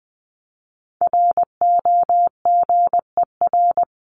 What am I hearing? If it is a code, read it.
Morse code, heard as a keyed tone, about 20 wpm, 700 Hz. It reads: ROGER
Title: CWT Exchange 20wpm